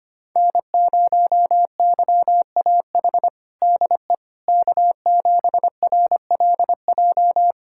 Morse code, heard as a keyed tone, about 25 wpm, 700 Hz. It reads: N0YAH DE K7RLJ
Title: USA Callsign DE Callsign - 5 Characters - 25wpm